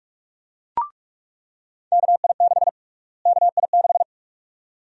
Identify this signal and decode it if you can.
Morse code, heard as a keyed tone, about 43 wpm, 700 Hz. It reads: KI6 KI6
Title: USA Call Sign Prefixes - Repeated 2x - 45wpm